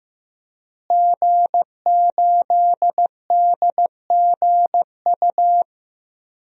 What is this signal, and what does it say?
Morse code, heard as a keyed tone, about 15 wpm, 700 Hz. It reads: G8DGU
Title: DXCC-100 Call Signs - 5 Characters - Repeated 3x - 15wpm